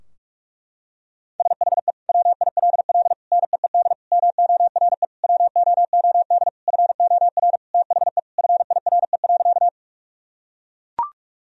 Morse code, heard as a keyed tone, about 41 wpm, 700 Hz. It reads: SHE WILL NEED MORE WOOD FOR THE FIRE.
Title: Sentences from Top 1000 Words - Review Part 2 - 45wpm